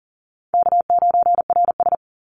Morse code, heard as a keyed tone, about 40 wpm, 700 Hz. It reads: K9RS
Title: Easy Call Signs - 40wpm